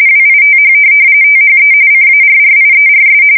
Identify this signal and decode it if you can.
RTTY, about 45.45 baud. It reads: WITH THE CARRIERS
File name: fsk170 / USS-Massachusetts